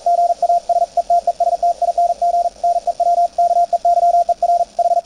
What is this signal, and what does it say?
Morse code, heard as a keyed tone, about 41 wpm, 690 Hz, with radio noise. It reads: GA SETESTING NEW KEYER H
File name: attachments / CW 20m 200Hz